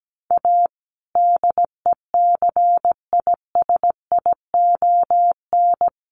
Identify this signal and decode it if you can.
Morse code, heard as a keyed tone, about 17 wpm, 700 Hz. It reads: A DECISION
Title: Sentences from Top 1100 Words 17wpm